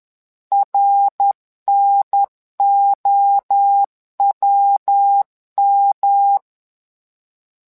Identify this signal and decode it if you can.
Morse code, heard as a keyed tone, about 10 wpm, 800 Hz. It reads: RNOWM
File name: LEZIONI / 3 MWRO / 3 MWRO 10wpm 2148